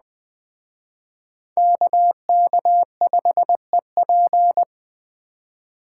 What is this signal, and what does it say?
Morse code, heard as a keyed tone, about 20 wpm, 700 Hz. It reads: KK5EP
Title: Speed Racing - USA 2x2 Call Signs 17wpm